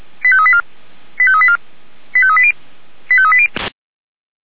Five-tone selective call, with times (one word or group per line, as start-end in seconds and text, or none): 0.2-0.6 85384
1.2-1.6 85384
2.1-2.5 85380
3.1-3.5 85380